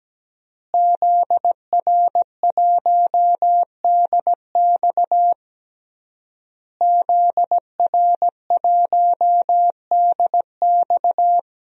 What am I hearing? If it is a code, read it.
Morse code, heard as a keyed tone, about 17 wpm, 700 Hz. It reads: ZR1DX ZR1DX
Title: DXCC-100 Call Signs - 5 Characters - Repeated 3x - 17wpm